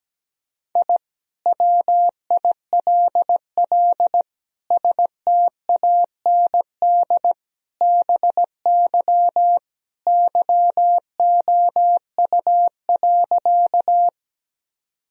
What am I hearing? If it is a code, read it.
Morse code, heard as a keyed tone, about 17 wpm, 700 Hz. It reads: I WILL STAND BY YOU.
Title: Sentences from Top 1000 Words - Review Part 1 - 17wpm